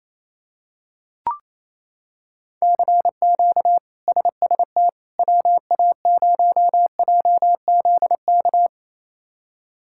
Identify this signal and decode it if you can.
Morse code, heard as a keyed tone, about 28 wpm, 700 Hz. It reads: CQ SST WA0JZK
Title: K1USN SST QSO Practice - 28wpm